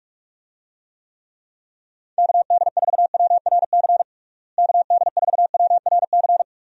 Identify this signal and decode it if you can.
Morse code, heard as a keyed tone, about 42 wpm, 700 Hz. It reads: KD4WRC KD4WRC
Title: USA Call Signs - 6 Characters - Repeated 3x - 45wpm